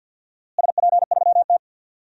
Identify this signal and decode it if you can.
Morse code, heard as a keyed tone, about 52 wpm, 700 Hz. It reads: SP3T
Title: DXCC-100 Call Signs - 4 Characters - Repeated 3x - 50wpm